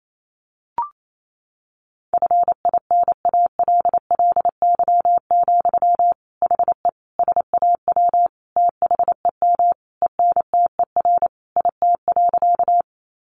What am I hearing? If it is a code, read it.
Morse code, heard as a keyed tone, about 28 wpm, 700 Hz. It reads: FINALLY, HE SAW THEM ENTER IT.
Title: Sentences from Top 1300 Words 28wpm